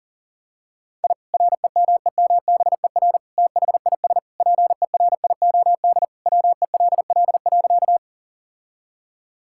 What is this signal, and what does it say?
Morse code, heard as a keyed tone, about 40 wpm, 700 Hz. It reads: I REMEMBER THIS PERIOD WELL.
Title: Sentences from Top 1000 Words - Review Part 5 - 40wpm